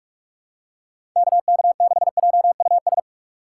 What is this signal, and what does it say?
Morse code, heard as a keyed tone, about 48 wpm, 700 Hz. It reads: KK6JUS
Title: USA Call Signs - 6 Characters - 8x Character Spacing - 45wpm